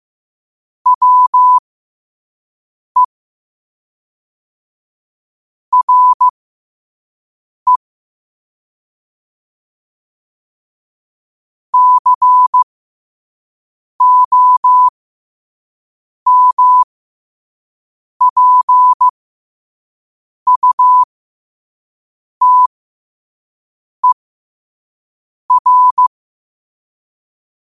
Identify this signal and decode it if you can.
Morse code, heard as a keyed tone, about 15 wpm, spaced as for 5 wpm, 1000 Hz. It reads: WE RE COMPUTER
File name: qotdfiles / 5wpm